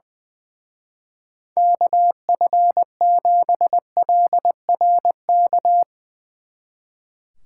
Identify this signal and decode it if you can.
Morse code, heard as a keyed tone, about 22 wpm, 700 Hz. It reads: KF7LRK
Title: Speed Racing - USA 2x3 Call Signs 20wpm